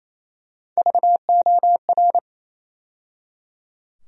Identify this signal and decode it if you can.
Morse code, heard as a keyed tone, about 28 wpm, 700 Hz. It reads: VOR
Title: USA Call Sign Postfixes - 3 Letters 28wpm